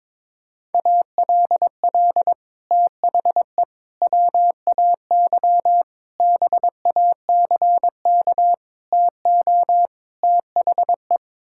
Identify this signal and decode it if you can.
Morse code, heard as a keyed tone, about 22 wpm, 700 Hz. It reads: ALL THE WAY BACK TO THE
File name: practice / 22wpm / Sets of 6 Words - 22wpm